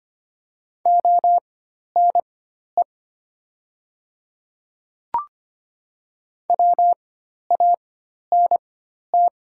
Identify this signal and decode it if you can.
Morse code, heard as a keyed tone, about 25 wpm, spaced as for 12 wpm, 700 Hz. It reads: ONE WANT
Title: 4x Character Spacing - Top 200 Words - 25wpm